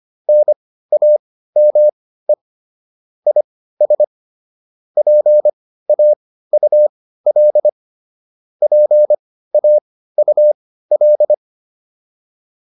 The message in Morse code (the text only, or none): NAME IS PAUL PAUL